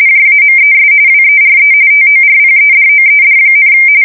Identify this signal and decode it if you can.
RTTY, about 45.45 baud. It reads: JAMES WERE TORPEDOED,
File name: fsk170 / usnavy4